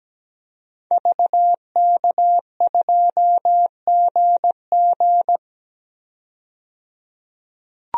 Morse code, heard as a keyed tone, about 17 wpm, 700 Hz. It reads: VK2GG
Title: DXCC-100 Call Signs - 5 Characters - 17wpm